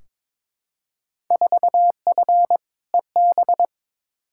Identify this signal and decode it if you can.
Morse code, heard as a keyed tone, about 21 wpm, 700 Hz. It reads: 4F EB
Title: Field Day Exchanges - 22wpm